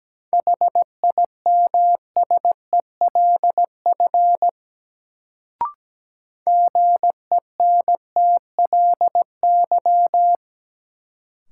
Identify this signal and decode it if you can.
Morse code, heard as a keyed tone, about 17 wpm, 700 Hz. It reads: HIMSELF GENTLY
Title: Sentences from Top 1500 Words - New Word Drill Alt - 17wpm